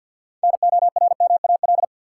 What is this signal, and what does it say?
Morse code, heard as a keyed tone, about 53 wpm, 700 Hz. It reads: NORMAL
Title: Six Letter Words - 50wpm